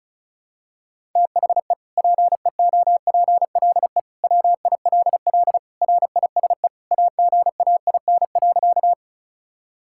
Morse code, heard as a keyed tone, about 35 wpm, 700 Hz. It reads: THE PEOPLE WILL RISE AGAIN.
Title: Sentences from Top 1000 Words - Review Part 6 - 35wpm